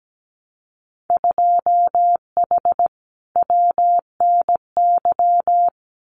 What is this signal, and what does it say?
Morse code, heard as a keyed tone, about 17 wpm, 700 Hz. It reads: 2H WNY
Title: Winter Field Day Exchanges Encore 17wpm